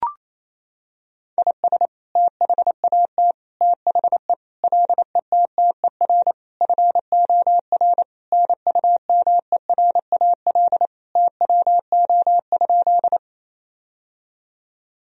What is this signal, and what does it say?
Morse code, heard as a keyed tone, about 28 wpm, 700 Hz. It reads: IS THAT THE LETTER FOR NUMERAL TWO?
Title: Sentences from Top 1000 Words - Review Part 2 - 28wpm